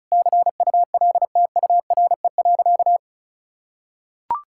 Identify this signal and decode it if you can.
Morse code, heard as a keyed tone, about 36 wpm, 700 Hz. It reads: CULTURE.
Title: Sentences from Top 1000 Words - Review Part 6 - 35wpm